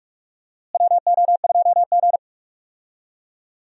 Morse code, heard as a keyed tone, about 46 wpm, 700 Hz. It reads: WO2G
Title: USA Call Signs - 4 Characters - Repeated 2x - 45wpm